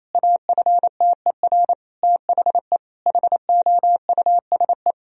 Morse code, heard as a keyed tone, about 27 wpm, 700 Hz. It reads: AFTER THE HOUSE
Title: Sentences from Top 1000 Words - Review Part 4 - 28wpm